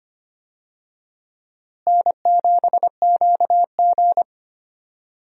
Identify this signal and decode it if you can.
Morse code, heard as a keyed tone, about 25 wpm, 700 Hz. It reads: N7QG
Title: USA Call Signs - 4 Characters - Repeated 2x - 25wpm